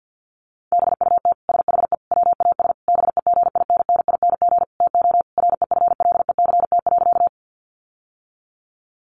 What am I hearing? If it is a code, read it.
Morse code, heard as a keyed tone, about 50 wpm, 700 Hz. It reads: BUT SHE WAS BEGINNING TO REFLECT.
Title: Sentences from Top 1500 Words 50wpm